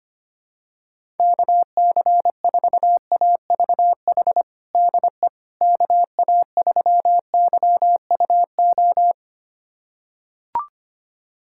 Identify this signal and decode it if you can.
Morse code, heard as a keyed tone, about 25 wpm, 700 Hz. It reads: KC4AVH DE KA3YUO
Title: USA Callsign DE Callsign - 6 Characters - 25wpm